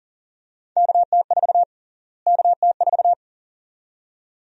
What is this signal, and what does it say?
Morse code, heard as a keyed tone, about 40 wpm, 700 Hz. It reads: KT4 KT4
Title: USA Call Sign Prefixes - Repeated 2x - 40wpm